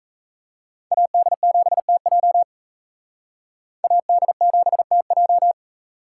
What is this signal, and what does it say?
Morse code, heard as a keyed tone, about 35 wpm, 700 Hz. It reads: AD7TJ AD7TJ
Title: Speed Racing - USA 2x2 Call Signs 28wpm